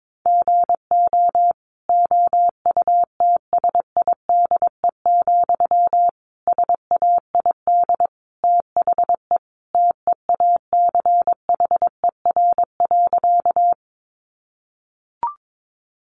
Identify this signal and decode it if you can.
Morse code, heard as a keyed tone, about 22 wpm, 700 Hz. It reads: GO OUTSIDE, SAID THE TEACHER.
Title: Sentences from Top 900 Words 22wpm